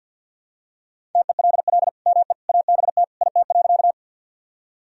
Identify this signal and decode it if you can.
Morse code, heard as a keyed tone, about 52 wpm, 700 Hz. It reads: TELL ME ABT IT.
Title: Sentences from Top 200 Words and Abbreviations 50wpm